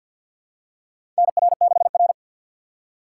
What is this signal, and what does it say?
Morse code, heard as a keyed tone, about 51 wpm, 700 Hz. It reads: NR6R
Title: USA Call Signs - 4 Characters - 6x Character Spacing - 50wpm